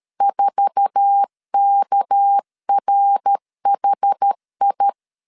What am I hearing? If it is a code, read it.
Morse code, heard as a keyed tone, about 13 wpm, 800 Hz. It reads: 4KRHI
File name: CW / DET / 1 9 1500C S A 12wpm 3800